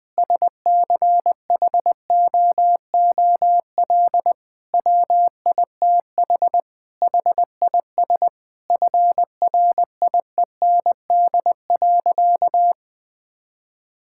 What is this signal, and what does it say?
Morse code, heard as a keyed tone, about 20 wpm, 700 Hz. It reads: SCHOOL WITH HIS FRIEND.
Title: Sentences from Top 1600 Words 20wpm